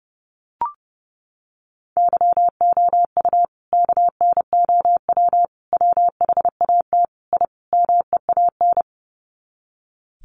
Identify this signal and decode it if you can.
Morse code, heard as a keyed tone, about 30 wpm, 700 Hz. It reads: YOU KNOW WHAT I MEAN
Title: Sets of 5 Words 30wpm